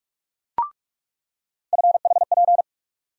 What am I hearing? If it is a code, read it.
Morse code, heard as a keyed tone, about 46 wpm, 700 Hz. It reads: FHP
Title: USA Call Sign Postfixes - 3 Letters 45wpm